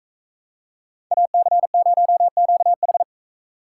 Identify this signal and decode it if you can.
Morse code, heard as a keyed tone, about 43 wpm, 700 Hz. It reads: AC0QH